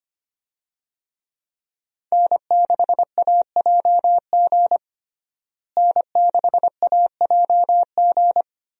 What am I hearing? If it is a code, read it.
Morse code, heard as a keyed tone, about 25 wpm, 700 Hz. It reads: N6AJG N6AJG
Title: USA Call Signs - 5 Characters - Repeated 2x - 25wpm